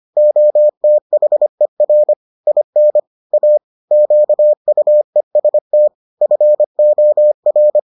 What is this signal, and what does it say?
Morse code, heard as a keyed tone, about 25 wpm, 600 Hz. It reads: OTHER IN A QUEST FOR